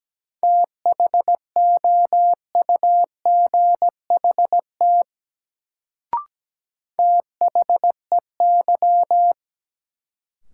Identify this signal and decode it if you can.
Morse code, heard as a keyed tone, about 17 wpm, 700 Hz. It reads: THOUGHT THEY
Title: Sets of 2 Words - New Word Drill - 17wpm